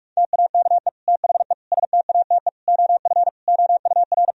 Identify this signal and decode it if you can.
Morse code, heard as a keyed tone, about 46 wpm, 700 Hz. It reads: TAKE THE STATE OF OUR
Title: Sentences from Top 1000 Words - Review Part 3 - 45wpm